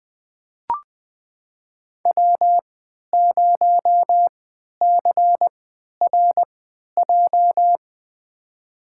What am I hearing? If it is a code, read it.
Morse code, heard as a keyed tone, about 20 wpm, spaced as for 11 wpm, 700 Hz. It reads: W0CRJ
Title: USA Call Signs - 3x Character Spacing - 20wpm